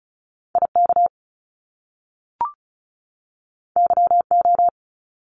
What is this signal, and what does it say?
Morse code, heard as a keyed tone, about 36 wpm, 700 Hz. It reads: IK YO
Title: Top Two Letter Combinations in Words 35wpm